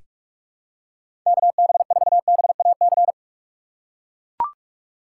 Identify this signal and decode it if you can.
Morse code, heard as a keyed tone, about 44 wpm, 700 Hz. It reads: KB4BAC
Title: USA Call Signs - 6 Characters - 6x Character Spacing - 45wpm